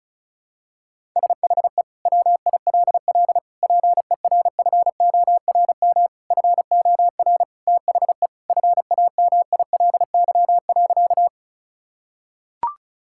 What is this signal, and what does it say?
Morse code, heard as a keyed tone, about 34 wpm, 700 Hz. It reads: SHE WILL PERFORM FOR THE FAMILY.
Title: Sentences from Top 1000 Words - Review Part 5 - 35wpm